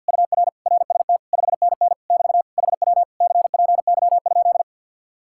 Morse code, heard as a keyed tone, about 65 wpm, 700 Hz. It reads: UR RST 5NN <BT> HW CPY?